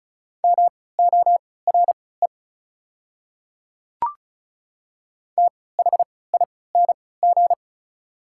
Morse code, heard as a keyed tone, about 36 wpm, spaced as for 20 wpm, 690 Hz. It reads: MORE THING
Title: 3x Character Spacing - Top 500 Words - 35wpm